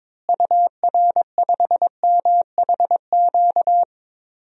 Morse code, heard as a keyed tone, about 22 wpm, 700 Hz. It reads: UR5MHQ
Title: DXCC-100 Call Signs - 6 Characters - Repeated 3x - 22wpm